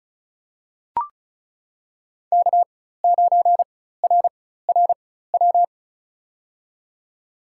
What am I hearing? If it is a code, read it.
Morse code, heard as a keyed tone, about 34 wpm, spaced as for 16 wpm, 700 Hz. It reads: K9RRW
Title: USA Call Signs - 5 Characters - 4x Character Spacing - 35wpm